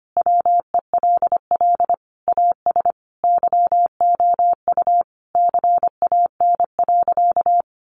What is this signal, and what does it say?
Morse code, heard as a keyed tone, about 25 wpm, 700 Hz. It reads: WELL AS YOU CAN.
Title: Sentences from Top 1400 Words 25wpm